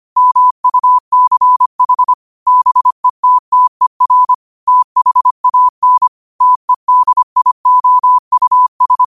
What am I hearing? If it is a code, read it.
Morse code, heard as a keyed tone, about 25 wpm, 1000 Hz. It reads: MUCH BETTER THAN TEDIOUS
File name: qotdfiles / 25wpm